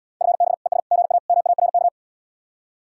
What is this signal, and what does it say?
Morse code, heard as a keyed tone, about 80 wpm, 700 Hz. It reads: FB ES GA MARK